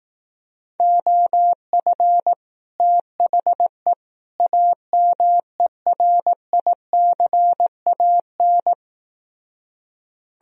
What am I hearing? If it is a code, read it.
Morse code, heard as a keyed tone, about 18 wpm, 700 Hz. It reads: OF THE AMERICAN
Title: Speed-Racing - Sets of 3 Words 15wpm